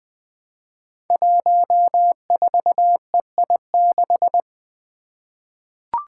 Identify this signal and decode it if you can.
Morse code, heard as a keyed tone, about 20 wpm, 700 Hz. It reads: 14EI6
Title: Five Challenging Letter-Number Combinations 20wpm